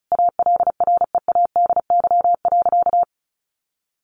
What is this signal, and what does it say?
Morse code, heard as a keyed tone, about 34 wpm, 700 Hz. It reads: ALREADY.
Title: Sentences from Top 900 Words 35wpm